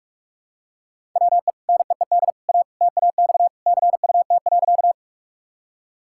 Morse code, heard as a keyed tone, about 46 wpm, 700 Hz. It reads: WE NEED A TAX CUT.
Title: Sentences from Top 1000 Words - Review Part 5 - 45wpm